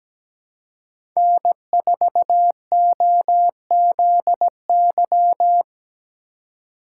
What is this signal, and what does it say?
Morse code, heard as a keyed tone, about 17 wpm, 700 Hz. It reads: N4OZY
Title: USA Call Signs - 5 Characters - 4x Character Spacing - 17wpm